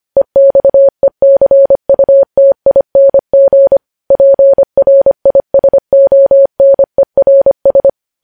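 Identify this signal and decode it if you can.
Morse code, heard as a keyed tone, about 25 wpm, 550 Hz. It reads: EXECUTING PRISONERS
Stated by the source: Morse code news headlines sent at 25 wpm.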